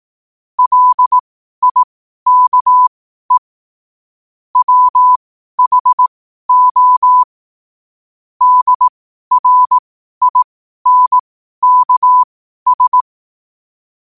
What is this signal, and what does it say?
Morse code, heard as a keyed tone, about 18 wpm, spaced as for 12 wpm, 1000 Hz. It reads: LIKE WHO DRINKS
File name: qotdfiles / 12wpm